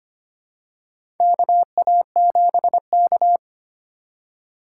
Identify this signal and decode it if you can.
Morse code, heard as a keyed tone, about 25 wpm, 700 Hz. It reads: KA7K
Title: USA Call Signs - 4 Characters - 2x Character Spacing - 25wpm